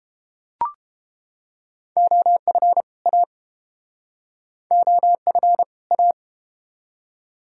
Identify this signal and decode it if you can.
Morse code, heard as a keyed tone, about 31 wpm, 700 Hz. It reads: OF A OF A